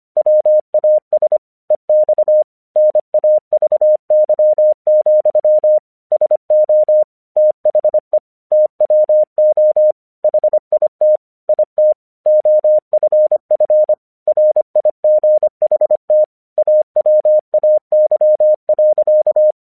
Morse code, heard as a keyed tone, about 25 wpm, 610 Hz. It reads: WAS EX NAVY, SO THE TWO HIT IT OFF RIGHT AWAY.